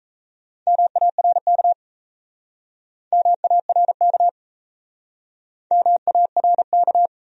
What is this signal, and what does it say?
Morse code, heard as a keyed tone, about 39 wpm, 700 Hz. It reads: MARK MARK MARK